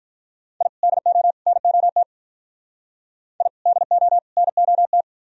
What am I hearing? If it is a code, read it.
Morse code, heard as a keyed tone, about 44 wpm, 700 Hz. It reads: I DO NOT I DO NOT